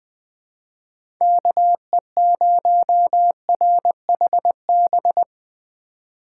Morse code, heard as a keyed tone, about 20 wpm, 700 Hz. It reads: KE0RHB
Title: USA Call Signs - 6 Characters - 3x Character Spacing - 20wpm